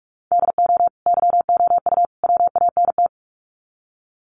Morse code, heard as a keyed tone, about 44 wpm, 700 Hz. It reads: DO YOU WANT
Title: Sets of 3 Words 45wpm